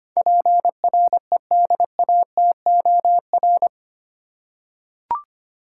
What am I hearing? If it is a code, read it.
Morse code, heard as a keyed tone, about 25 wpm, 700 Hz. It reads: PREDATOR
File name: 8 Letter Words - First 3 Letters Given - 25wpm